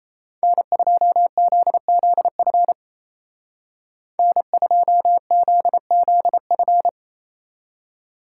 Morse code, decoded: N2ZZF N2ZZF

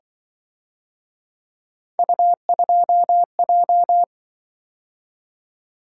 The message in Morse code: U2J

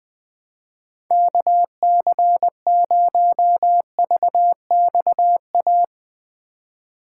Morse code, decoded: KC0VXA

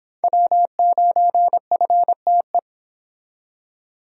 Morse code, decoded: W9FTE

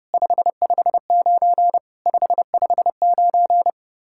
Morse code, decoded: 559 559

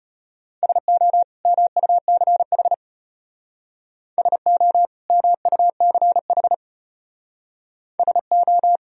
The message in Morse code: SO MUCH SO MUCH SO

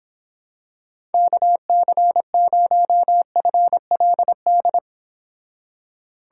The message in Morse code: KC0FLD

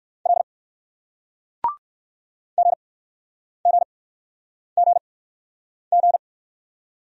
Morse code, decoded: L GGGG

